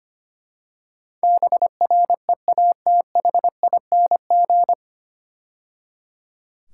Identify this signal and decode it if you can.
Morse code, heard as a keyed tone, about 25 wpm, 700 Hz. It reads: BREATHING
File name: practice / 25wpm / 9 Letter Words - First 3 Letters Given - 25wpm